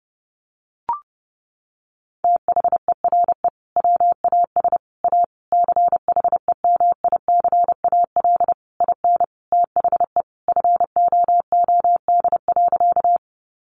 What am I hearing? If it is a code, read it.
Morse code, heard as a keyed tone, about 30 wpm, 700 Hz. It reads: THERE WAS A CHEMICAL IN THE FOOD.
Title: Sentences from Top 1200 Words 30wpm